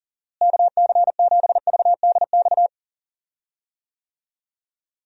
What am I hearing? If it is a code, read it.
Morse code, heard as a keyed tone, about 39 wpm, 700 Hz. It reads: KC7VDX
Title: USA Call Signs - 6 Characters - Repeated 3x - 40wpm